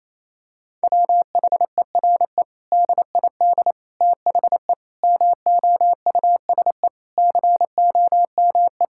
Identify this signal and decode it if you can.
Morse code, heard as a keyed tone, about 27 wpm, 700 Hz. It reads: WHERE DID THE MOUSE COME
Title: Sentences from Top 1000 Words - Review Part 5 - 28wpm